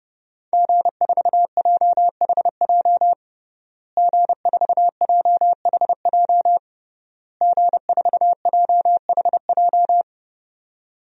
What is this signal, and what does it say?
Morse code, heard as a keyed tone, about 30 wpm, 700 Hz. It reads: G4JHJ G4JHJ G4JHJ
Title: DXCC-100 Call Signs - 5 Characters - Repeated 3x - 30wpm